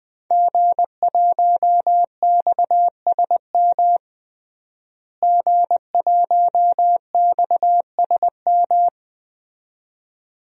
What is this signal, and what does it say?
Morse code, heard as a keyed tone, about 20 wpm, 700 Hz. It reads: G1XSM G1XSM
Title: DXCC-100 Call Signs - 5 Characters - Repeated 2x - 20wpm